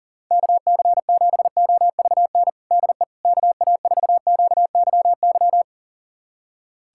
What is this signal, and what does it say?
Morse code, heard as a keyed tone, about 40 wpm, 700 Hz. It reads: KC7OVN DE KA4QYY
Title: USA Callsign DE Callsign 40wpm